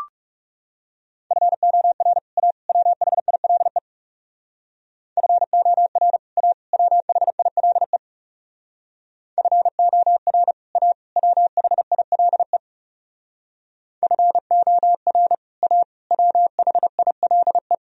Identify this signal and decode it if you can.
Morse code, heard as a keyed tone, about 37 wpm, 700 Hz. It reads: FOR A WHILE FOR A WHILE FOR A WHILE FOR A WHILE
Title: Speed-Racing - Sets of 3 Words 30wpm